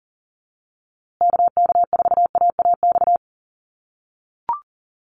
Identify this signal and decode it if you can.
Morse code, heard as a keyed tone, about 40 wpm, 700 Hz. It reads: KK4AAX
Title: USA Call Signs Repeated 3x 40wpm